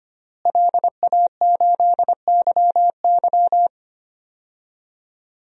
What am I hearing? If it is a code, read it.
Morse code, heard as a keyed tone, about 25 wpm, 700 Hz. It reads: LA8YY
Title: DXCC-100 Call Signs - 5 Characters - Repeated 3x - 25wpm